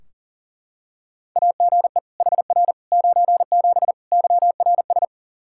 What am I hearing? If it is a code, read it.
Morse code, heard as a keyed tone, about 37 wpm, 700 Hz. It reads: AGE HR 97 YRS
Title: Practice QSO - Exchange 3 - More Info - 40wpm